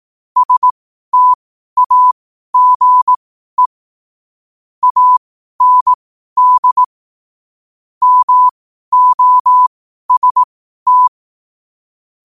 Morse code, heard as a keyed tone, about 18 wpm, spaced as for 12 wpm, 1000 Hz. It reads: STAGE AND MOST
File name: qotdfiles / 12wpm